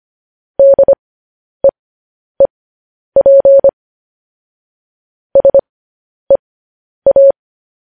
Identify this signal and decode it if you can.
Morse code, heard as a keyed tone, about 25 wpm, spaced as for 10 wpm, 550 Hz. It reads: DEEP SEA
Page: news headlines in morse code at 10 wpm